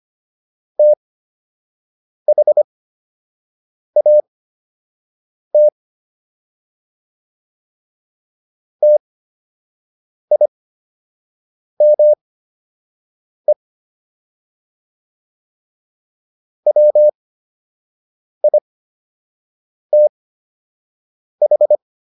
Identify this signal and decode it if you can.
Morse code, heard as a keyed tone, about 25 wpm, spaced as for 6 wpm, 600 Hz. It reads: THAT TIME WITH